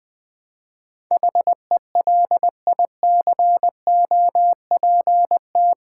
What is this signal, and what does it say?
Morse code, heard as a keyed tone, about 20 wpm, 700 Hz. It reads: HELICOPT
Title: Build A Word - 4 Letters At A Time - 10-Letter Words - 20wpm